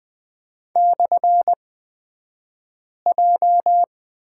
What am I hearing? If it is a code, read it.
Morse code, heard as a keyed tone, about 20 wpm, spaced as for 5 wpm, 700 Hz. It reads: /J
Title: 242 Lesson 39d - Letter z - Two Characters 20wpm